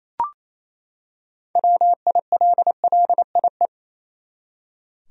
Morse code, heard as a keyed tone, about 28 wpm, 700 Hz. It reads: WILLIE
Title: USA Common Names - 28wpm